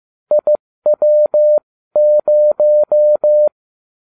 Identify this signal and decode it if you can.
Morse code, heard as a keyed tone, about 15 wpm, spaced as for 13 wpm, 600 Hz. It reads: IW0